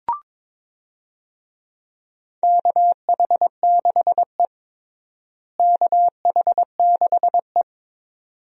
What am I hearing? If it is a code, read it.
Morse code, heard as a keyed tone, about 21 wpm, 700 Hz. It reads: KH6E KH6E
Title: DXCC-100 Call Signs - 4 Characters - Repeated 2x - 22wpm